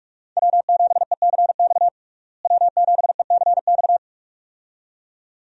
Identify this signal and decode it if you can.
Morse code, heard as a keyed tone, about 46 wpm, 700 Hz. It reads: W7ECX W7ECX
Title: USA Call Signs - 5 Characters - Repeated 2x - 45wpm